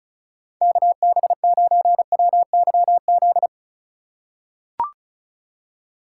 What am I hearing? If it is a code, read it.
Morse code, heard as a keyed tone, about 36 wpm, 700 Hz. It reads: KB9WYZ
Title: USA Call Signs - 6 Characters - 4x Character Spacing - 35wpm